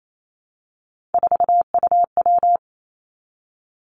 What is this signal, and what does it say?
Morse code, heard as a keyed tone, about 28 wpm, 700 Hz. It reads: 4UW
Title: Three Challenging Letter-Number Combinations 28wpm